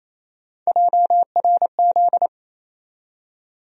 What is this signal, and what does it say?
Morse code, heard as a keyed tone, about 28 wpm, 700 Hz. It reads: JRZ
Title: USA Call Sign Postfixes - 3 Letters 28wpm